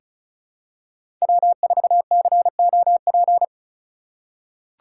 Morse code, heard as a keyed tone, about 34 wpm, 690 Hz. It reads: W4COP
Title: USA Call Signs - 5 Characters - 3x Character Spacing - 35wpm